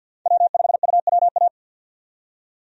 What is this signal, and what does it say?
Morse code, heard as a keyed tone, about 49 wpm, 700 Hz. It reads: W5UWA
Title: USA Call Signs - 5 Characters - 50wpm